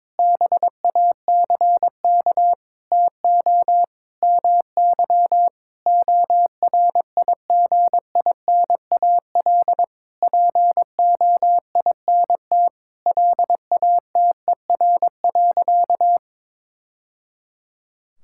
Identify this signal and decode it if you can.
Morse code, heard as a keyed tone, about 22 wpm, 700 Hz. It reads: BACK TO MY ORIGINAL POINT LATER.
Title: Sentences from Top 1000 Words - Review Part 6 - 22wpm